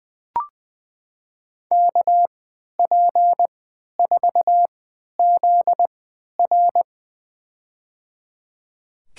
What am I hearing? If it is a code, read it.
Morse code, heard as a keyed tone, about 20 wpm, spaced as for 11 wpm, 700 Hz. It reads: KP4ZR
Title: USA Call Signs - 5 Characters - 3x Character Spacing - 20wpm